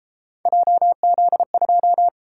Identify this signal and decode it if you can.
Morse code, heard as a keyed tone, about 32 wpm, 700 Hz. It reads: JZ2